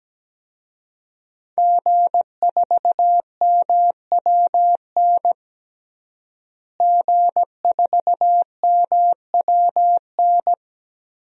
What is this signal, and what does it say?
Morse code, heard as a keyed tone, about 17 wpm, 700 Hz. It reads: G4MWN G4MWN
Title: DXCC-100 Call Signs - 5 Characters - Repeated 3x - 17wpm